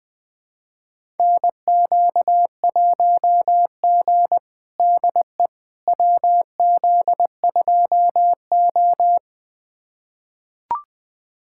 Morse code, heard as a keyed tone, about 20 wpm, 700 Hz. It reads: NQ1G DE WZ2O
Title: USA Callsign DE Callsign - 4 Characters - 20wpm